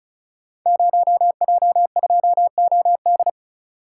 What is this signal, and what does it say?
Morse code, heard as a keyed tone, about 36 wpm, 700 Hz. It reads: 0J2OD
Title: Code Groups - 5 Letter:Numbers - 35wpm